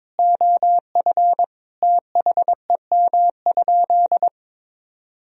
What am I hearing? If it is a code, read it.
Morse code, heard as a keyed tone, about 21 wpm, 700 Hz. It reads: OF THEM?
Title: Sentences from Top 1000 Words - Review Part 4 - 22wpm